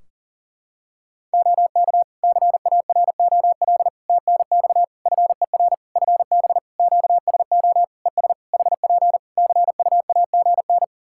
Morse code, heard as a keyed tone, about 40 wpm, 700 Hz. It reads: OK CAROL TNX FER FB QSO ES HP CUAGN
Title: K1USN SST QSO Practice 40wpm